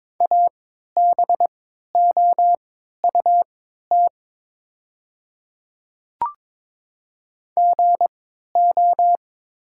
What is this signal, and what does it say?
Morse code, heard as a keyed tone, about 22 wpm, spaced as for 12 wpm, 700 Hz. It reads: ABOUT GO
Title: 3x Character Spacing - Top 200 Words - 22wpm